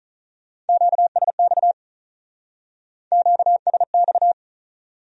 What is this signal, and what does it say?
Morse code, heard as a keyed tone, about 41 wpm, 700 Hz. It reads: QSX QSX
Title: Speed-Racing Q-Codes - 35wpm